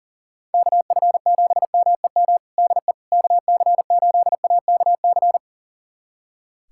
Morse code, heard as a keyed tone, about 40 wpm, 700 Hz. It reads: KF7MEM DE KC8AKC
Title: USA Callsign DE Callsign - 6 Characters - 40wpm